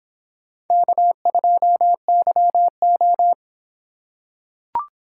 Morse code, heard as a keyed tone, about 25 wpm, 700 Hz. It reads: K2YO